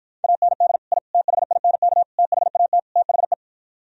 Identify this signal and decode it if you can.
Morse code, heard as a keyed tone, about 52 wpm, 700 Hz. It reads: AND I THINK THAT THE